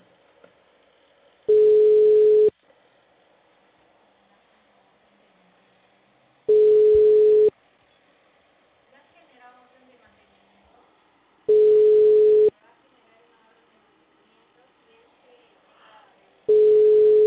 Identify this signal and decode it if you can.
Ringback tone (Europe, 425 Hz), with speech.